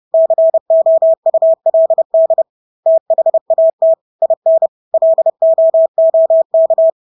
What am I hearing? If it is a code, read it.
Morse code, heard as a keyed tone, about 30 wpm, 650 Hz. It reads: COULD THAT IN LOOK
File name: Engelske ord 30-30 wpm